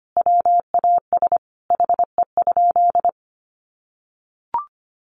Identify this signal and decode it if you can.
Morse code, heard as a keyed tone, about 25 wpm, 700 Hz. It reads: WAS HE?
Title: Sentences from Top 1300 Words 25wpm